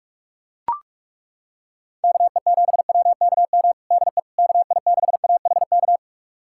Morse code, heard as a keyed tone, about 42 wpm, 700 Hz. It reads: KE7WKM DE KI6AHK
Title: USA Callsign DE Callsign - 6 Characters - 45wpm